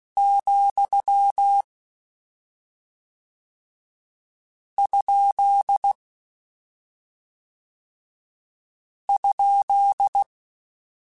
Morse code, heard as a keyed tone, about 16 wpm, 800 Hz. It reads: , ? ?
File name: Misc / Misc-mp3s / Punct @10